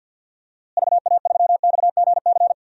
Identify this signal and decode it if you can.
Morse code, heard as a keyed tone, about 52 wpm, 700 Hz. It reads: VA3XGC
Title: DXCC-100 Call Signs - 6 Characters - Repeated 3x - 50wpm